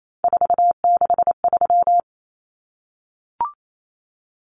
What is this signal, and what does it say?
Morse code, heard as a keyed tone, about 28 wpm, 700 Hz. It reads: 463
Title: Common Cut Numbers - 3-Digit 28wpm